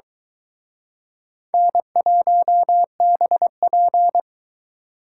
Morse code, heard as a keyed tone, about 23 wpm, 700 Hz. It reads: N1BP